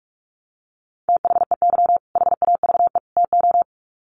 Morse code, heard as a keyed tone, about 44 wpm, 700 Hz. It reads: THEY HAVE TO
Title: Sets of 3 Words 45wpm